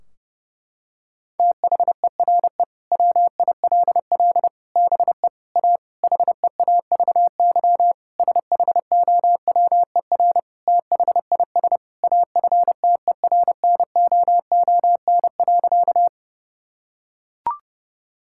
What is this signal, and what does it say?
Morse code, heard as a keyed tone, about 30 wpm, 700 Hz. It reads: THERE WILL BE A HEAVY SHOWER THIS AFTERNOON.
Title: Sentences from Top 1600 Words 30wpm